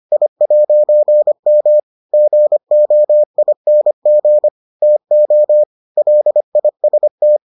Morse code, heard as a keyed tone, about 25 wpm, 600 Hz. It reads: I'M GOING TO LIST